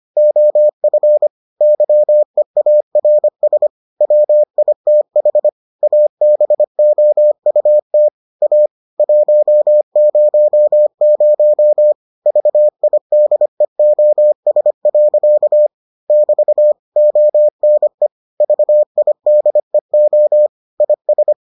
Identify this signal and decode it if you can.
Morse code, heard as a keyed tone, about 25 wpm, 600 Hz. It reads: OF YEARS WITH ABOUT A 100 VIDEOS. <BT> ONE VIDEO IS